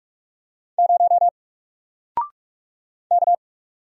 Morse code, heard as a keyed tone, about 46 wpm, 700 Hz. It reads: 0 K
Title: Single Character (Letter:Number:Punctuation:Prosign) 45wpm